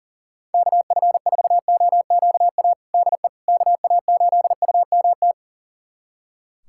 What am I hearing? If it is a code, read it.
Morse code, heard as a keyed tone, about 40 wpm, 700 Hz. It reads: KF4OQA DE KA8UMT
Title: USA Callsign DE Callsign - 6 Characters - 40wpm